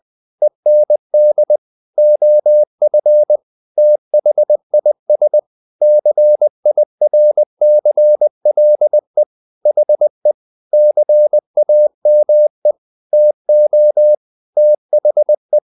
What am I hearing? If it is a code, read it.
Morse code, heard as a keyed tone, about 20 wpm, 600 Hz. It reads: END OF THIS CIRCLE HE CAME TO THE
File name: the old man and the sea part 2 20wpm